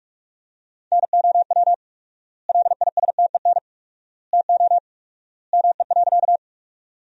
Morse code, heard as a keyed tone, about 46 wpm, 700 Hz. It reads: NOW LISTEN TO ME.
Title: 4x Word-Spacing - Sentences from Top 400 Words - 45wpm